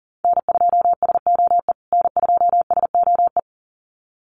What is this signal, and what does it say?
Morse code, heard as a keyed tone, about 40 wpm, 700 Hz. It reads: N2SOE N2SOE
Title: USA Call Signs Repeated 2x 40wpm